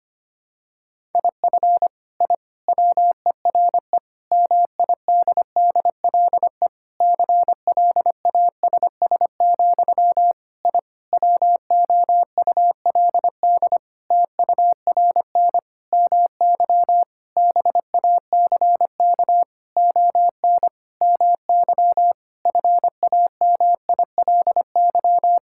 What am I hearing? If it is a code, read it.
Morse code, heard as a keyed tone, about 25 wpm, 700 Hz. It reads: IF I WERE MIDDLE CLASS, I WOULD TURN MY BACK ON MY FAMILY
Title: Sentences from Top 1000 Words - Review Part 6 - 25wpm